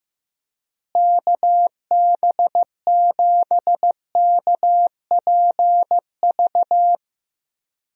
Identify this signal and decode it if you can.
Morse code, heard as a keyed tone, about 15 wpm, 700 Hz. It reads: KB7KPV
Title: USA Call Signs - 6 Characters - Repeated 3x - 15wpm